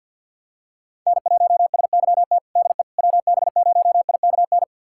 Morse code, heard as a keyed tone, about 52 wpm, 700 Hz. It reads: N1SYT DE WB0IKN
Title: USA Callsign DE Callsign 50wpm